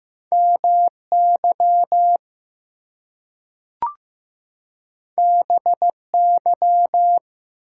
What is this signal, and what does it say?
Morse code, heard as a keyed tone, about 15 wpm, 700 Hz. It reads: MY BY
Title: Two Letter Words - 15wpm